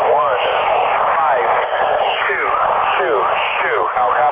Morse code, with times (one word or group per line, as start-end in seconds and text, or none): none